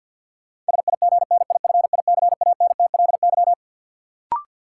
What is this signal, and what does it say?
Morse code, heard as a keyed tone, about 52 wpm, 700 Hz. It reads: SIGNIFICANTLY
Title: Sentences from Top 1500 Words - New Word Drill Alt - 50wpm